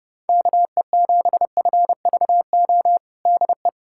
Morse code, heard as a keyed tone, about 30 wpm, 700 Hz. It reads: KE7FVO DE
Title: USA Callsign DE Callsign - 6 Characters - 30wpm